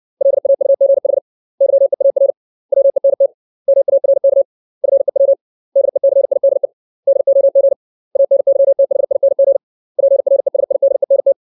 Morse code, heard as a keyed tone, about 60 wpm, 550 Hz. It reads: LAUGH YEAR WENT MARK FEW DECIDE DOG ANYTHING PRESIDENT
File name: 632 words / 632 Words 60WPM